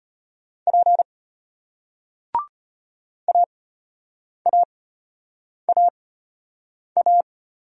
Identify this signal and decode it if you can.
Morse code, heard as a keyed tone, about 33 wpm, spaced as for 8 wpm, 700 Hz. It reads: P AAAA